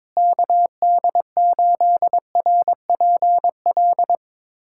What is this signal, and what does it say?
Morse code, heard as a keyed tone, about 22 wpm, 700 Hz. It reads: KD8RPL
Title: USA Callsign DE Callsign 22wpm